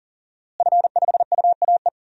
Morse code, heard as a keyed tone, about 39 wpm, 700 Hz. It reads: F5UAE
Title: DXCC-100 Call Signs - 5 Characters - Repeated 3x - 40wpm